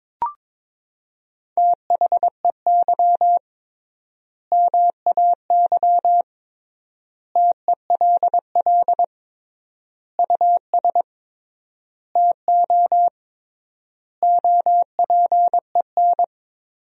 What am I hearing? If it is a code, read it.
Morse code, heard as a keyed tone, about 21 wpm, 700 Hz. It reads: THEY MAY TELL US TO OPEN
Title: 3x Word-Spacing - Sentences from Top 300 Words - 22wpm